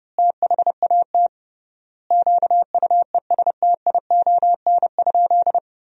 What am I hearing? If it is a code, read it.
Morse code, heard as a keyed tone, about 32 wpm, 700 Hz. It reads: THAT QUESTION?
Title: 3x Word-Spacing - Sentences from Top 400 Words - 30wpm